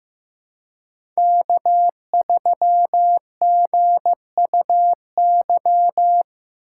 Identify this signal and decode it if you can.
Morse code, heard as a keyed tone, about 15 wpm, 700 Hz. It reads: K3GUY
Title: USA Call Signs - 5 Characters - 4x Character Spacing - 15wpm